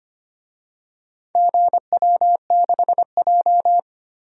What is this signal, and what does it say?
Morse code, heard as a keyed tone, about 25 wpm, 700 Hz. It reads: GW6J
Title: DXCC-100 Call Signs - 4 Characters - 25wpm